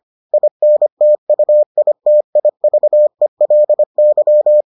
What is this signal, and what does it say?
Morse code, heard as a keyed tone, about 25 wpm, 600 Hz. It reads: INTUITIVELY